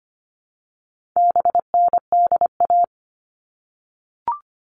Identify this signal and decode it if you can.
Morse code, heard as a keyed tone, about 25 wpm, 700 Hz. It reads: BNDA
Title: All Cut Numbers - 4-Digit 25wpm